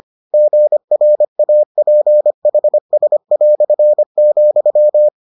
Morse code, heard as a keyed tone, about 25 wpm, 600 Hz. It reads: GRAPHS",